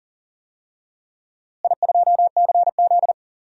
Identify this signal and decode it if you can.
Morse code, heard as a keyed tone, about 40 wpm, 700 Hz. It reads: I2CZ
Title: DXCC-100 Call Signs - 4 Characters - Repeated 3x - 40wpm